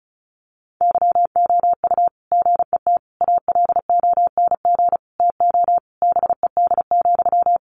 Morse code, heard as a keyed tone, about 34 wpm, 700 Hz. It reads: YOU GET ALONG TO BED,
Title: Sentences from Top 700 Words 35wpm